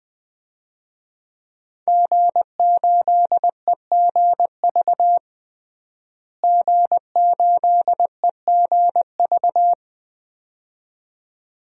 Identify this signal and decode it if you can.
Morse code, heard as a keyed tone, about 20 wpm, 700 Hz. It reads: G8EGV G8EGV
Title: DXCC-100 Call Signs - 5 Characters - Repeated 2x - 20wpm